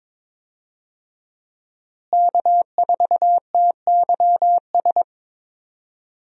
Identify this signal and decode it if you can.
Morse code, heard as a keyed tone, about 22 wpm, 700 Hz. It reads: K4TYS
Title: USA Call Signs - 5 Characters - Repeated 2x - 22wpm